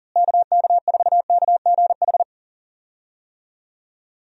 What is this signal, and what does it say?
Morse code, heard as a keyed tone, about 40 wpm, 700 Hz. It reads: KK4KGH
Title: USA Call Signs - 6 Characters - Repeated 2x - 40wpm